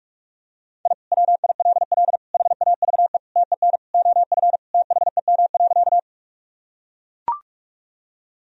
Morse code, heard as a keyed tone, about 42 wpm, 700 Hz. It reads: I WILL HAVE TEN OF THEM.
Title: Sentences from Top 1000 Words - Review Part 2 - 45wpm